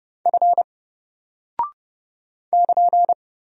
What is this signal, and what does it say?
Morse code, heard as a keyed tone, about 30 wpm, 700 Hz. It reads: F <KN>
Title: Single Character (Letter:Number:Punctuation:Prosign) Alt 30wpm